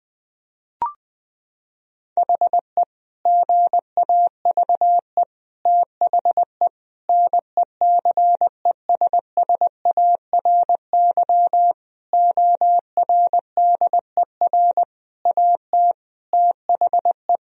Morse code, heard as a keyed tone, about 21 wpm, 700 Hz. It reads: HE GAVE THE NECESSARY ORDER AT THE
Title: Sentences from Top 1000 Words - Review Part 2 - 20wpm